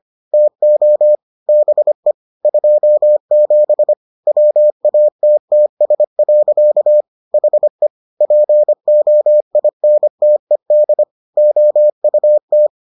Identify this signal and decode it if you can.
Morse code, heard as a keyed tone, about 25 wpm, 600 Hz. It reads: TO BE 27 WATTS. HE POINTED OUT